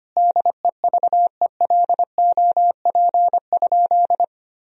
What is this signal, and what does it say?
Morse code, heard as a keyed tone, about 25 wpm, 700 Hz. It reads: DEVELOP?
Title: Sentences from Top 1000 Words - Review Part 2 - 25wpm